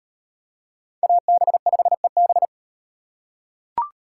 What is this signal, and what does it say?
Morse code, heard as a keyed tone, about 34 wpm, 700 Hz. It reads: AB5EB